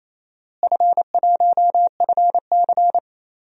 Morse code, heard as a keyed tone, about 28 wpm, 700 Hz. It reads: F1FC
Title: DXCC-100 Call Signs - 4 Characters - Repeated 2x - 28wpm